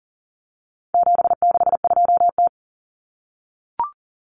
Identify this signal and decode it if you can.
Morse code, heard as a keyed tone, about 40 wpm, 700 Hz. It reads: 762T
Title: Common Cut Numbers - 4-Digit 40wpm